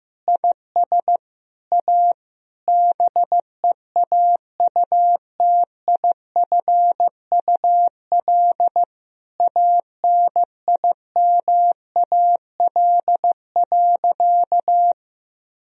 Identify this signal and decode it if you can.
Morse code, heard as a keyed tone, about 16 wpm, 700 Hz. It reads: IS A BEAUTIFUL ANIMAL.
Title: Sentences from Top 1000 Words - Review Part 1 - 15wpm